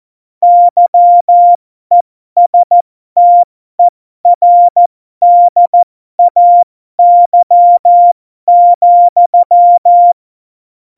Morse code, heard as a keyed tone, about 14 wpm, spaced as for 12 wpm, 700 Hz. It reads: YESTERDAY,
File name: Morsekurs / Witze